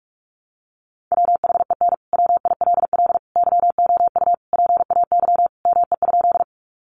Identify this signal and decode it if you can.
Morse code, heard as a keyed tone, about 44 wpm, 700 Hz. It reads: WHEN WILL YOU PAY ME?
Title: Sentences from Top 700 Words 45wpm